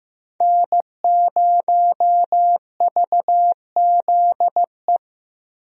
Morse code, heard as a keyed tone, about 15 wpm, 700 Hz. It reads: N0VZE
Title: USA Call Signs - 5 Characters - Repeated 3x - 15wpm